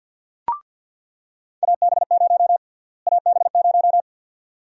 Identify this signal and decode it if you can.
Morse code, heard as a keyed tone, about 47 wpm, 700 Hz. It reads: AB0 AB0
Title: USA Call Sign Prefixes - Repeated 2x - 50wpm